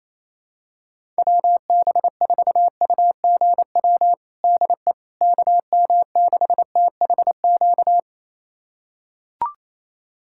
Morse code, heard as a keyed tone, about 28 wpm, 700 Hz. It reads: WB4UGW DE KM6THQ
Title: USA Callsign DE Callsign - 6 Characters - 28wpm